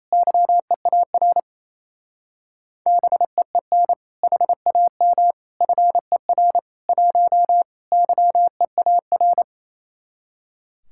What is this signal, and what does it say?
Morse code, heard as a keyed tone, about 36 wpm, 700 Hz. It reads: YEAR BEEN HAM FER 1 YEAR